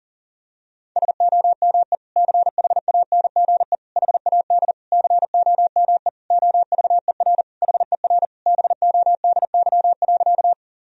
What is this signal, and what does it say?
Morse code, heard as a keyed tone, about 40 wpm, 700 Hz. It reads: SOME CHANGE HAD COME OVER HER BODY.
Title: Sentences from Top 1000 Words - Review Part 2 - 40wpm